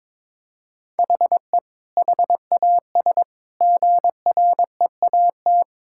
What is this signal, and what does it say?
Morse code, heard as a keyed tone, about 22 wpm, 700 Hz. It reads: HE HAS GREAT
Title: Sentences from Top 1000 Words - Review Part 6 - 22wpm